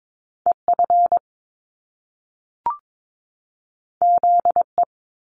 Morse code, heard as a keyed tone, about 22 wpm, 700 Hz. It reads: EF ZE
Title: Top Two Letter Combinations in Words 22wpm